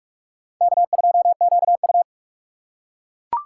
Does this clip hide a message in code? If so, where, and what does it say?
Yes: Morse code from 0.6 to 2.0 s, heard as a keyed tone, about 41 wpm, 700 Hz. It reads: K2QU